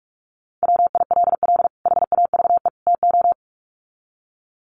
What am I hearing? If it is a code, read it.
Morse code, heard as a keyed tone, about 44 wpm, 700 Hz. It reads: WILL HAVE TO
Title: Sets of 3 Words 45wpm